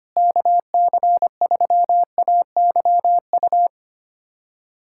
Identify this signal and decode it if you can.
Morse code, heard as a keyed tone, about 25 wpm, 700 Hz. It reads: KC3AYU
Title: USA Callsign DE Callsign 25wpm